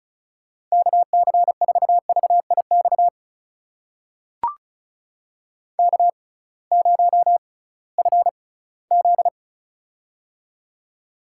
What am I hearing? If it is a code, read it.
Morse code, heard as a keyed tone, about 36 wpm, spaced as for 21 wpm, 700 Hz. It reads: KC4VIX K0FZ
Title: USA Call Signs - 6x Character Spacing - 35wpm